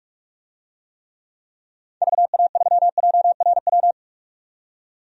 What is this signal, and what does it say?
Morse code, heard as a keyed tone, about 46 wpm, 700 Hz. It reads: VA3JRW
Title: DXCC-100 Call Signs - 6 Characters - 45wpm